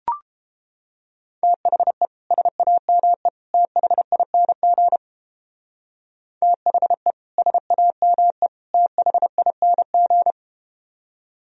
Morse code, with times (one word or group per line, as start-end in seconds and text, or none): 1.4-2.1 THE
2.3-3.3 SAME
3.5-5.0 THING
6.4-7.1 THE
7.4-8.5 SAME
8.7-10.3 THING